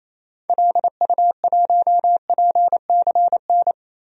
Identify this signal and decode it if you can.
Morse code, heard as a keyed tone, about 28 wpm, 700 Hz. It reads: LU1PCN
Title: DXCC-100 Call Signs - 6 Characters - Repeated 3x - 28wpm